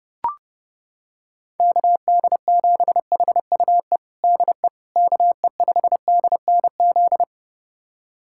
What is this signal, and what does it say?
Morse code, heard as a keyed tone, about 32 wpm, 700 Hz. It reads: KD7HUE DE KE5DNZ
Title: USA Callsign DE Callsign - 6 Characters - 30wpm